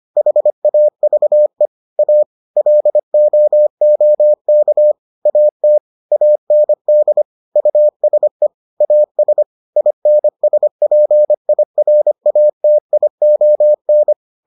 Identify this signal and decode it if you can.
Morse code, heard as a keyed tone, about 25 wpm, 600 Hz. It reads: HAVE A LOOK AT AND USE AS INSPIRATION